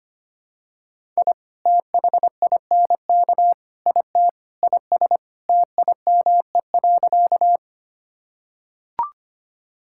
Morse code, heard as a keyed tone, about 25 wpm, 700 Hz. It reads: I THINK IT IS TIME.
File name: practice / 25wpm / Sentences with 4-Word Phrases - 25wpm